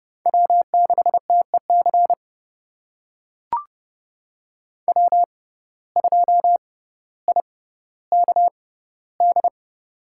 Morse code, heard as a keyed tone, about 30 wpm, spaced as for 15 wpm, 700 Hz. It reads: W6TEC W2IKD
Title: USA Call Signs - 5 Characters - 6x Character Spacing - 30wpm